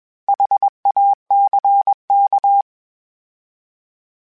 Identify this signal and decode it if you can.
Morse code, heard as a keyed tone, about 21 wpm, 800 Hz. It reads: HACK
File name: H Prefixes / H mp3s